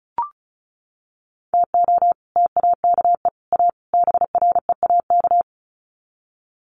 Morse code, heard as a keyed tone, about 34 wpm, 700 Hz. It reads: TO TAKE A BREAK
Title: Sets of 4 Words 35wpm